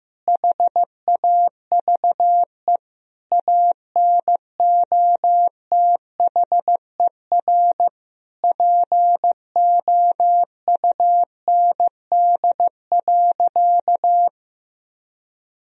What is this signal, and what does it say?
Morse code, heard as a keyed tone, about 16 wpm, 700 Hz. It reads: HAVE ANOTHER POUND.
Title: Sentences from Top 1000 Words - Review Part 2 - 15wpm